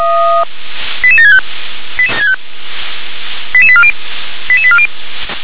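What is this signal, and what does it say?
Five-tone selective call (ZVEI), heard as quick runs of five tones, tones of about 70 ms. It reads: 88765 88840 88840
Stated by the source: ZVEI five-tone call 88983 received by SDR.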